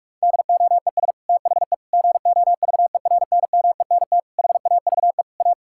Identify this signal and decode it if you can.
Morse code, heard as a keyed tone, about 46 wpm, 700 Hz. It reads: DOES THE GOVERNMENT HAVE A
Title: Sentences from Top 1600 Words 45wpm